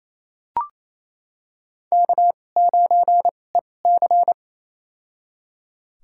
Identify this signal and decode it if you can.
Morse code, heard as a keyed tone, about 28 wpm, spaced as for 20 wpm, 700 Hz. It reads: K9EC
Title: USA Call Signs - 4 Characters - 2x Character Spacing - 28wpm